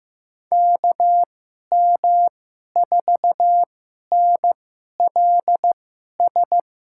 Morse code, heard as a keyed tone, about 15 wpm, spaced as for 11 wpm, 700 Hz. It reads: KM4NLS
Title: USA Call Signs - 6 Characters - 2x Character Spacing - 15wpm